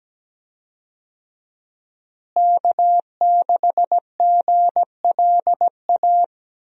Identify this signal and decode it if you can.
Morse code, heard as a keyed tone, about 17 wpm, 700 Hz. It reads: K6GLA
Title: USA Call Signs - 5 Characters - Repeated 3x - 17wpm